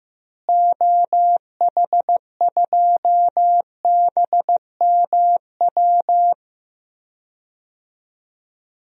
Morse code, heard as a keyed tone, about 15 wpm, 700 Hz. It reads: OH2BMW
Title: DXCC-100 Call Signs - 6 Characters - Repeated 2x - 15wpm